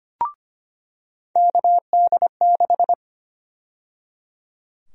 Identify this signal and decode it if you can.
Morse code, heard as a keyed tone, about 25 wpm, 700 Hz. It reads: KD6
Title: USA Call Sign Prefixes - 25wpm